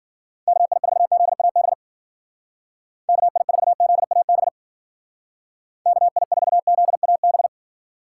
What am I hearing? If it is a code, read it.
Morse code, heard as a keyed tone, about 50 wpm, 700 Hz. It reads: KI4ZAB KI4ZAB KI4ZAB
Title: Speed Racing - USA 2x3 Call Signs 40wpm